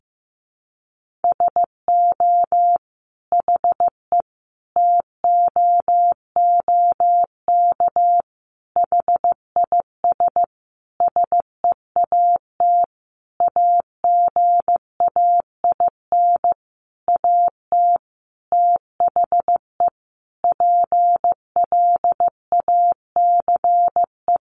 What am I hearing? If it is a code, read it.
Morse code, heard as a keyed tone, about 15 wpm, 700 Hz. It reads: SO HE TOOK HIS SEAT AGAIN AT THE PLACE
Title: Sentences from Top 1500 Words 15wpm